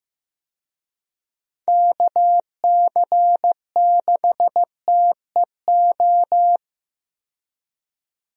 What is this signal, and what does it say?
Morse code, heard as a keyed tone, about 15 wpm, 700 Hz. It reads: KC6TEO
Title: USA Call Signs - 6 Characters - 15wpm